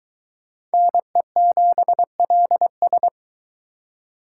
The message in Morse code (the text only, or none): NE7LS